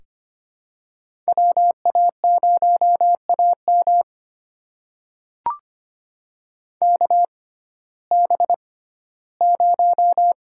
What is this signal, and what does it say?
Morse code, heard as a keyed tone, about 25 wpm, spaced as for 15 wpm, 700 Hz. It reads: WA0AM KB0
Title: USA Call Signs - 5 Characters - 6x Character Spacing - 25wpm